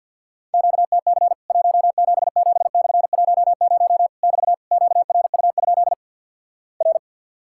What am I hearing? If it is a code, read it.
Morse code, heard as a keyed tone, about 50 wpm, 700 Hz. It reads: QTC 177/10 <BT> QRU? R
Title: Worked All Europe DX Contest - QTC Exchange - 50wpm